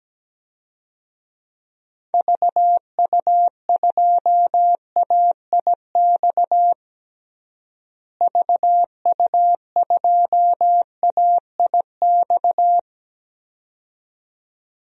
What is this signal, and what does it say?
Morse code, heard as a keyed tone, about 17 wpm, 700 Hz. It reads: VU2AIX VU2AIX
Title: DXCC-100 Call Signs - 6 Characters - Repeated 2x - 17wpm